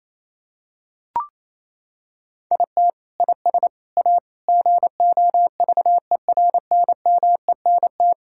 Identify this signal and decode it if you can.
Morse code, heard as a keyed tone, about 28 wpm, 700 Hz. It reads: IT IS A GOVERNMENT
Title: Sentences from Top 1000 Words - Review Part 5 - 28wpm